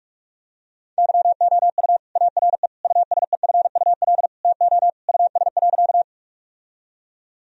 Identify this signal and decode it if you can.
Morse code, heard as a keyed tone, about 46 wpm, 700 Hz. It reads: YOU ARE USEFUL TO US.
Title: Sentences from Top 1000 Words - Review Part 4 - 45wpm